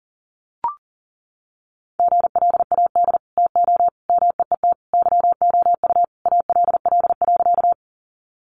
Morse code, heard as a keyed tone, about 40 wpm, 700 Hz. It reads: GLAD TO MEET YOU ALL.
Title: Sentences from Top 1100 Words 40wpm